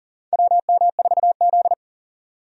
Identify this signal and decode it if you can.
Morse code, heard as a keyed tone, about 40 wpm, 700 Hz. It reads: WM4Z
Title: USA Call Signs - 4 Characters - 40wpm